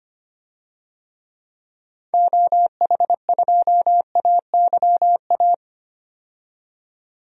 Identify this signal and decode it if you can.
Morse code, heard as a keyed tone, about 25 wpm, 700 Hz. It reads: OH2AYA
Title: DXCC-100 Call Signs - 6 Characters - 25wpm